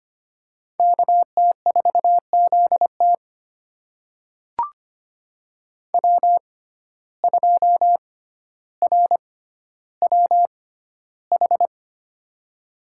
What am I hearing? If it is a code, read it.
Morse code, heard as a keyed tone, about 25 wpm, spaced as for 13 wpm, 700 Hz. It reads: KT4ZT W2RWH
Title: USA Call Signs - 5 Characters - 6x Character Spacing - 25wpm